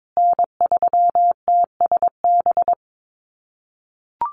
Morse code, heard as a keyed tone, about 22 wpm, 700 Hz. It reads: N3TSB
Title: USA Call Signs Repeated 2x 22wpm